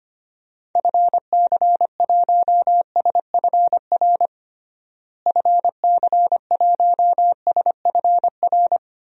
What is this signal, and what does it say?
Morse code, heard as a keyed tone, about 25 wpm, 700 Hz. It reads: FC1SFR FC1SFR
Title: DXCC-100 Call Signs - 6 Characters - Repeated 3x - 25wpm